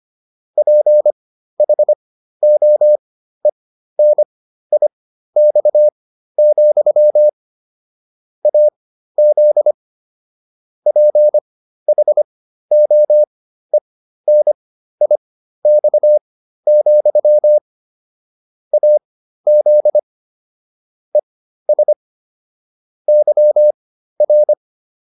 Morse code, heard as a keyed tone, about 25 wpm, spaced as for 13 wpm, 600 Hz. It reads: PHOENIX, AZ PHOENIX, AZ ES YR